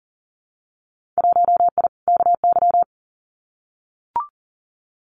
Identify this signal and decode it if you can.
Morse code, heard as a keyed tone, about 40 wpm, 700 Hz. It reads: 1I KY
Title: Winter Field Day Exchanges 40wpm